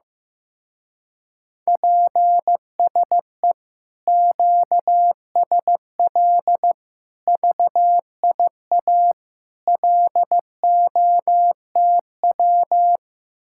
Morse code, heard as a keyed tone, about 15 wpm, 700 Hz. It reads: PSE QSL VIA LOTW
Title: Speed-Racing Common QSO Phrases Encore - 15wpm